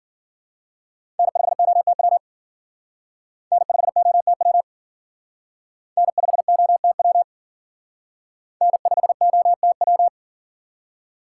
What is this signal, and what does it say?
Morse code, heard as a keyed tone, about 52 wpm, 700 Hz. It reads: N5OTW N5OTW N5OTW N5OTW